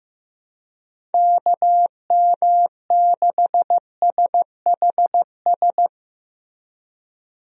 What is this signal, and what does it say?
Morse code, heard as a keyed tone, about 15 wpm, 700 Hz. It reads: KM6SHS
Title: USA Call Signs - 6 Characters - Repeated 2x - 15wpm